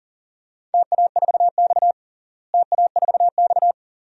Morse code, heard as a keyed tone, about 38 wpm, 700 Hz. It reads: TA4X TA4X
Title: DXCC-100 Call Signs - 4 Characters - Repeated 2x - 40wpm